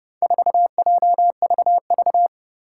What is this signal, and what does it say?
Morse code, heard as a keyed tone, about 30 wpm, 700 Hz. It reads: 4JVV
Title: Four Challenging Letter-Number Combinations 30wpm